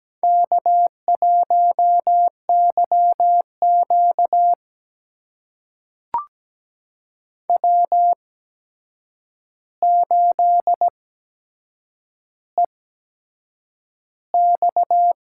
Morse code, heard as a keyed tone, about 17 wpm, spaced as for 7 wpm, 700 Hz. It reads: K1YQ W8EX
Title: USA Call Signs - 4 Characters - 8x Character Spacing - 17wpm